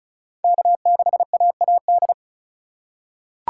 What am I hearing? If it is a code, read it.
Morse code, heard as a keyed tone, about 36 wpm, 700 Hz. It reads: K6AAD
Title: USA Call Signs - 5 Characters - 8x Character Spacing - 35wpm